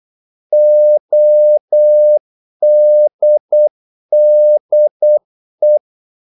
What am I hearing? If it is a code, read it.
Morse code, heard as a keyed tone, about 8 wpm, 600 Hz. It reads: ODDE
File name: Korte ord DK 8wpm